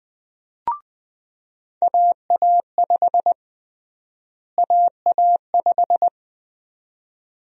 Morse code, heard as a keyed tone, about 20 wpm, 700 Hz. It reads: AA5 AA5
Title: USA Call Sign Prefixes - Repeated 2x - 20wpm